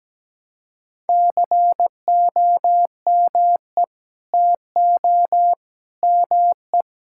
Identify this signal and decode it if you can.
Morse code, heard as a keyed tone, about 17 wpm, 700 Hz. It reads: COME TO ME
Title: Sentences from Top 1000 Words - Review Part 6 - 17wpm